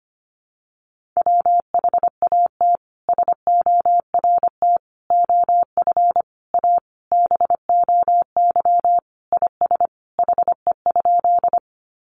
Morse code, heard as a keyed tone, about 25 wpm, 700 Hz. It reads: WHAT SORT OF A BOY IS HE?
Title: Sentences from Top 1300 Words 25wpm